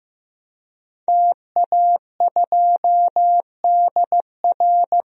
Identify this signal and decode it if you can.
Morse code, heard as a keyed tone, about 15 wpm, 700 Hz. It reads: TA2DR
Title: DXCC-100 Call Signs - 5 Characters - Repeated 2x - 15wpm